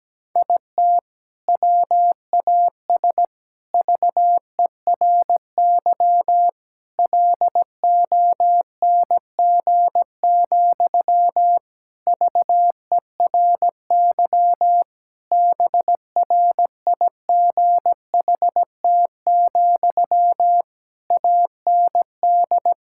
Morse code, heard as a keyed tone, about 17 wpm, 700 Hz. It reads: IT WAS VERY LONG, VERY BRIGHT, AND
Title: Sentences from Top 1600 Words 17wpm